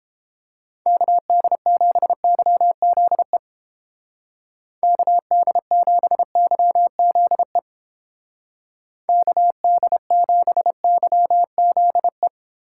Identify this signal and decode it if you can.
Morse code, heard as a keyed tone, about 29 wpm, 700 Hz. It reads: KD7YZE KD7YZE KD7YZE